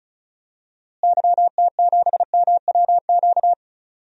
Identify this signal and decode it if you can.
Morse code, heard as a keyed tone, about 36 wpm, 700 Hz. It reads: YT7MWQ
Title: DXCC-100 Call Signs - 6 Characters - Repeated 3x - 35wpm